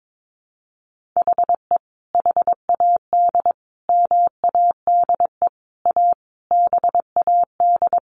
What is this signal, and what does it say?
Morse code, heard as a keyed tone, about 22 wpm, 700 Hz. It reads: HE HAD MADE A BAD
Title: Sentences from Top 1500 Words 22wpm